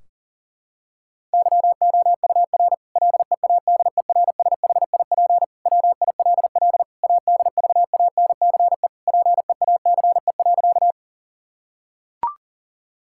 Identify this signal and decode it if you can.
Morse code, heard as a keyed tone, about 40 wpm, 700 Hz. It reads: YOUR LEADERSHIP WILL ADVANCE PEACE.
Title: Sentences from Top 1600 Words 40wpm